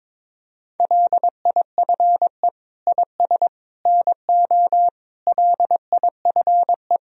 Morse code, heard as a keyed tone, about 22 wpm, 700 Hz. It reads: LIFE IS NO LIFE
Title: Sentences from Top 1000 Words - Review Part 3 - 22wpm